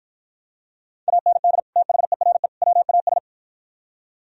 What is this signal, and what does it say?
Morse code, heard as a keyed tone, about 49 wpm, 700 Hz. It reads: AND THERE WAS